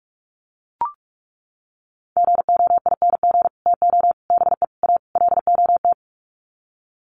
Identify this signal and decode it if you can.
Morse code, heard as a keyed tone, about 44 wpm, 700 Hz. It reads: GOING TO BE A LOT
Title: Sets of 5 Words Encore 45wpm